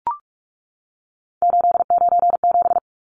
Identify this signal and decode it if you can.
Morse code, heard as a keyed tone, about 44 wpm, 700 Hz. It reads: 897
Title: Numbers - 3-Digit 45wpm